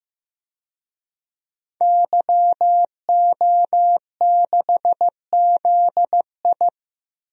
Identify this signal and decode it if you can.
Morse code, heard as a keyed tone, about 16 wpm, 700 Hz. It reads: YO6ZI
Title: DXCC-100 Call Signs - 5 Characters - Repeated 2x - 15wpm